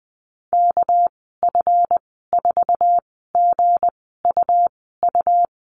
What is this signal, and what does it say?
Morse code, heard as a keyed tone, about 20 wpm, spaced as for 14 wpm, 700 Hz. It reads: KF4GUU
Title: 213 Lesson 34g - Number 8 - Call Signs 20wpm